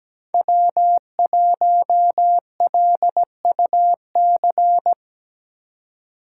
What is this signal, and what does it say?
Morse code, heard as a keyed tone, about 17 wpm, 700 Hz. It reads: W1LUC
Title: USA Call Signs - 5 Characters - 8x Character Spacing - 17wpm